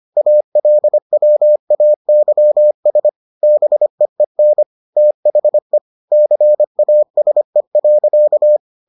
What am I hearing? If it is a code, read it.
Morse code, heard as a keyed tone, about 25 wpm, 600 Hz. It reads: ALWAYS BEEN THE CASE.